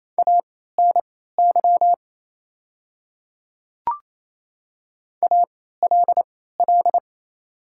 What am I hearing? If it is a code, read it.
Morse code, heard as a keyed tone, about 28 wpm, spaced as for 16 wpm, 700 Hz. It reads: ANY ALL
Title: 3x Character Spacing - Top 200 Words - 28wpm